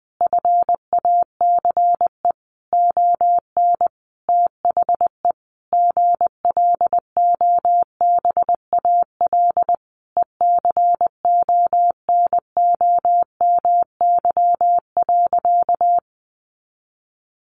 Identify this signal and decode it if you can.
Morse code, heard as a keyed tone, about 20 wpm, 700 Hz. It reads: FACE ON THE GLOBAL ECONOMY.
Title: Sentences from Top 1200 Words 20wpm